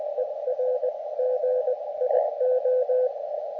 Morse code, heard as a keyed tone, about 20 wpm, spaced as for 17 wpm, 510 Hz, with radio noise. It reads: ERGIO
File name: Expedition / SWL UT5NM 20.12.2012 VANGUARD UA1CEG p IT9IFI